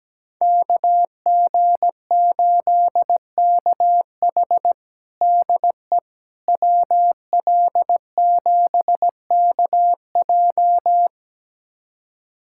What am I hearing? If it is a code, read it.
Morse code, heard as a keyed tone, about 17 wpm, 700 Hz. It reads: KG8KH DE WL7KJ
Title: USA Callsign DE Callsign - 5 Characters - 17wpm